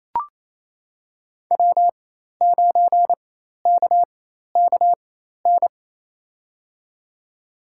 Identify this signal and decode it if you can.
Morse code, heard as a keyed tone, about 27 wpm, spaced as for 13 wpm, 700 Hz. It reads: W9KKN
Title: USA Call Signs - 5 Characters - 4x Character Spacing - 28wpm